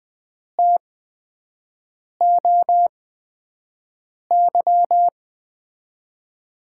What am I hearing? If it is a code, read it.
Morse code, heard as a keyed tone, about 21 wpm, spaced as for 5 wpm, 700 Hz. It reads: TOY
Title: USA Call Signs - 6 Characters - 8x Character Spacing - 20wpm